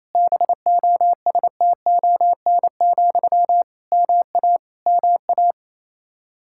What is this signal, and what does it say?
Morse code, heard as a keyed tone, about 28 wpm, 700 Hz. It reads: BOSTON, MA MA
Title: Practice QSO - Exchange 1 - Beginning - 28wpm